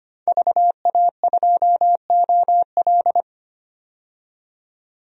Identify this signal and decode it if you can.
Morse code, heard as a keyed tone, about 25 wpm, 700 Hz. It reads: VA2OL
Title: DXCC-100 Call Signs - 5 Characters - 25wpm